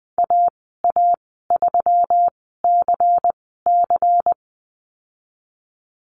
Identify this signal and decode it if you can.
Morse code, heard as a keyed tone, about 20 wpm, spaced as for 15 wpm, 700 Hz. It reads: AA3CC
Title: 140 Lesson 23g - Letter f - Call Signs 20wpm